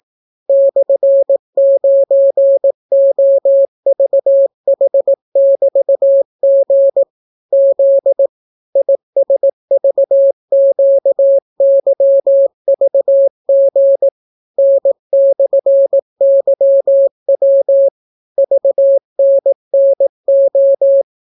Morse code, heard as a keyed tone, about 18 wpm, 550 Hz. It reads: /9OVH<BT>G Z ISVQYVG N/YW VNNO